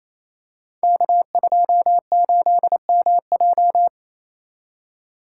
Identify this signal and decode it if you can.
Morse code, heard as a keyed tone, about 28 wpm, 690 Hz. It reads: K28MJ
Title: Code Groups - 5 Letter:Numbers - 28wpm